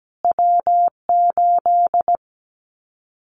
Morse code, heard as a keyed tone, about 17 wpm, 700 Hz. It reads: W8
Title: Callsign Prefixes for Top 25 DXCC - Repeated 3x - 17wpm